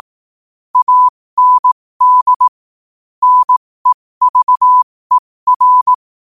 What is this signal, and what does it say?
Morse code, heard as a keyed tone, about 18 wpm, spaced as for 15 wpm, 1000 Hz. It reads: AND NEVER
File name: qotdfiles / 15wpm